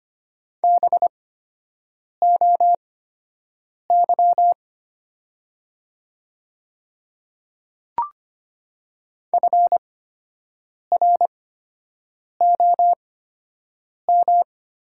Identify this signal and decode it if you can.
Morse code, heard as a keyed tone, about 25 wpm, spaced as for 7 wpm, 700 Hz. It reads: BOY FROM
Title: 8x Character Spacing - Top 200 Words - 25wpm